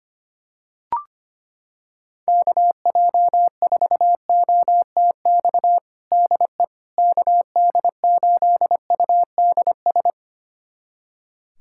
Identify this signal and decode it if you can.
Morse code, heard as a keyed tone, about 25 wpm, 700 Hz. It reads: KJ4OTX DE KD8UDS
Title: USA Callsign DE Callsign - 6 Characters - 25wpm